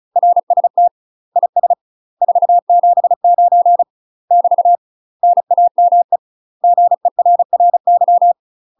Morse code, heard as a keyed tone, about 35 wpm, 700 Hz. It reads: RST IS 479 <BT> NAME GERRY
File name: UK MASTER QSOs TRAINING / 35WPM / UK MASTER QSOs 0002